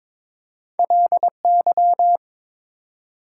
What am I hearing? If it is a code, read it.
Morse code, heard as a keyed tone, about 22 wpm, 700 Hz. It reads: LY
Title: Prefixes and Suffixes 22wpm